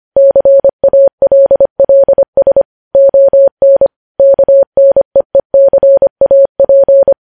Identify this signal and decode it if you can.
Morse code, heard as a keyed tone, about 25 wpm, 550 Hz. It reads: CALLS ON KNEECAP
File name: podcast / News headlines-1x-25x25